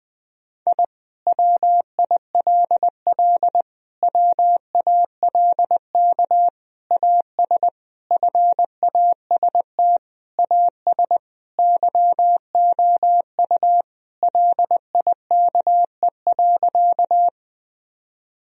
Morse code, heard as a keyed tone, about 20 wpm, 700 Hz. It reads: I WILL WALK AS FAST AS YOU LIKE.
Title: Sentences from Top 1000 Words - Review Part 2 - 20wpm